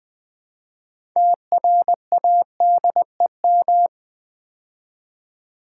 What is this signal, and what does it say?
Morse code, heard as a keyed tone, about 20 wpm, 700 Hz. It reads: TRADEM
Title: Build A Word - 3 Letters At A Time - 9-Letter Words - 20wpm